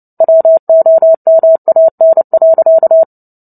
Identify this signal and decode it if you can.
Morse code, heard as a keyed tone, about 30 wpm, 650 Hz. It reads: WOMAN.
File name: Book Two / 03-Chapter 3 - 29 WPM